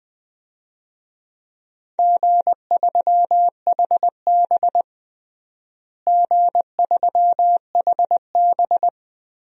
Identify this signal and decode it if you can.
Morse code, heard as a keyed tone, about 20 wpm, 700 Hz. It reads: G3HB G3HB
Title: DXCC-100 Call Signs - 4 Characters - Repeated 2x - 20wpm